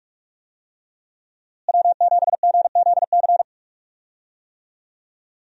Morse code, heard as a keyed tone, about 47 wpm, 700 Hz. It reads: W7GZC
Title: USA Call Signs - 5 Characters - Repeated 2x - 45wpm